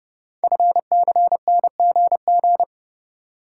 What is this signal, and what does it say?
Morse code, heard as a keyed tone, about 30 wpm, 700 Hz. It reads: FCNGG
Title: Code Groups - 5 Letters - 30wpm